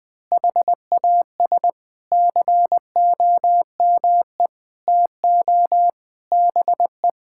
Morse code, heard as a keyed tone, about 20 wpm, 700 Hz. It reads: HAS COME TO BE
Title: Sentences with 8-Word Phrases - 20wpm